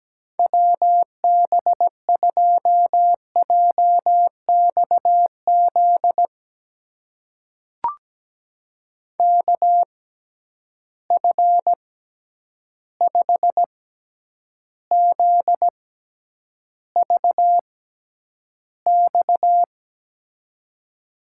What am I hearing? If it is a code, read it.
Morse code, heard as a keyed tone, about 17 wpm, spaced as for 9 wpm, 700 Hz. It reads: WB2JXZ KF5ZVX
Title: USA Call Signs - 6x Character Spacing - 17wpm